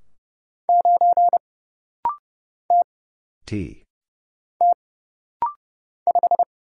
Morse code, heard as a keyed tone, about 30 wpm, 700 Hz. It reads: 9 T T 5
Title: Single Letter-Number - Rapid-Fire 30wpm